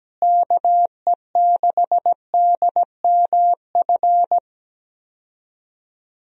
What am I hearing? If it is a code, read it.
Morse code, heard as a keyed tone, about 17 wpm, 700 Hz. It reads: KE6DMF
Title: USA Call Signs - 6 Characters - Repeated 3x - 17wpm